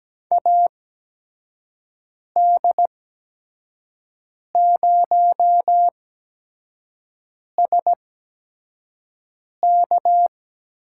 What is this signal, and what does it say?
Morse code, heard as a keyed tone, about 17 wpm, spaced as for 5 wpm, 700 Hz. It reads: AD0SK
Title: USA Call Signs - 5 Characters - 8x Character Spacing - 17wpm